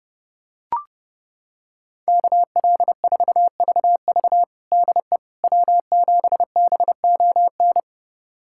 Morse code, heard as a keyed tone, about 32 wpm, 700 Hz. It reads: KL4VV DE W7BON
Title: USA Callsign DE Callsign - 5 Characters - 30wpm